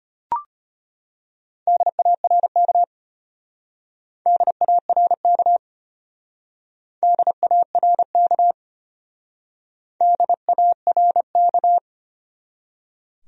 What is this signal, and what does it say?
Morse code, heard as a keyed tone, about 31 wpm, 700 Hz. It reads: DARK DARK DARK DARK
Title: Speed-Racing Top 500 Words 25wpm